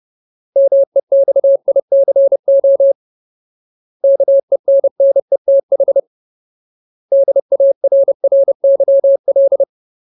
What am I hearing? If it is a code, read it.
Morse code, heard as a keyed tone, about 30 wpm, 550 Hz. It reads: MEXICO KENNETH DARRYL
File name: Ultimate Word List / 30WPM Ultimate